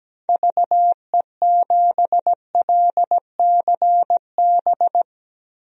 Morse code, heard as a keyed tone, about 17 wpm, 700 Hz. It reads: VE7LCB
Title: DXCC-100 Call Signs - 6 Characters - Repeated 3x - 17wpm